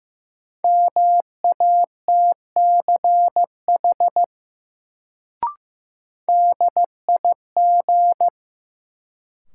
Morse code, heard as a keyed tone, about 15 wpm, 700 Hz. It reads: MATCH DIG
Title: Sentences from Top 1600 Words - New Word Drill - 15wpm